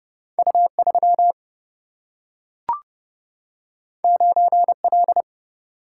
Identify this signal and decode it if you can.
Morse code, heard as a keyed tone, about 30 wpm, 700 Hz. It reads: U3 9L
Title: Single-Letter Single-Number 30wpm